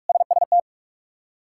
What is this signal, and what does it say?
Morse code, heard as a keyed tone, about 41 wpm, 700 Hz. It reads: SST
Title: K1USN SST QSO Practice - 45wpm